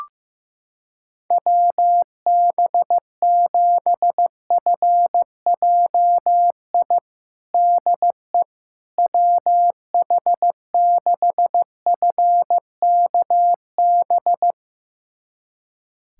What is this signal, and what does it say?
Morse code, heard as a keyed tone, about 16 wpm, 700 Hz. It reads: WB7FJI DE WH6FKB
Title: USA Callsign DE Callsign - 6 Characters - 15wpm